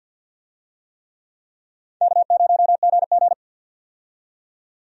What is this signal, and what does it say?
Morse code, heard as a keyed tone, about 52 wpm, 700 Hz. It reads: K0GG
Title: USA Call Signs - 4 Characters - 50wpm